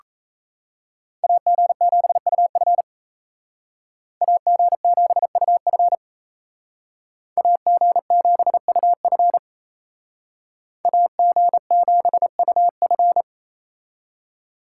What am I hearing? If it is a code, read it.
Morse code, heard as a keyed tone, about 36 wpm, 700 Hz. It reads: AG7UF AG7UF AG7UF AG7UF